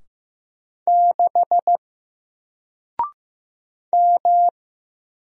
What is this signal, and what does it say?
Morse code, heard as a keyed tone, about 15 wpm, 700 Hz. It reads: 6 M
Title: Single Letter-Number 15wpm